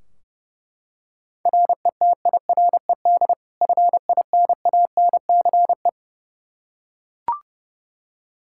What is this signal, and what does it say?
Morse code, heard as a keyed tone, about 30 wpm, 700 Hz. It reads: RETIRED FINANCE
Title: Practice QSO - Exchange 3 - More Info - 30wpm